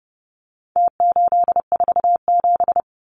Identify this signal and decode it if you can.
Morse code, heard as a keyed tone, about 29 wpm, 700 Hz. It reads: T847
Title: Numbers - Contest Style - 4-Digit 30wpm